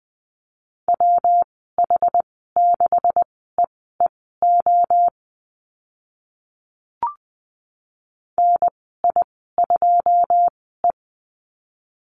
Morse code, heard as a keyed tone, about 20 wpm, spaced as for 14 wpm, 700 Hz. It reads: WH6EEO NI2E
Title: 167 Lesson 27g - Number 7 - Call Signs 20wpm